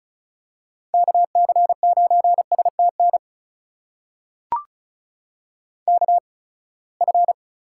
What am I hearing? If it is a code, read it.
Morse code, heard as a keyed tone, about 34 wpm, spaced as for 24 wpm, 700 Hz. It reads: KC9STN KF
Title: USA Call Signs - 6 Characters - 8x Character Spacing - 35wpm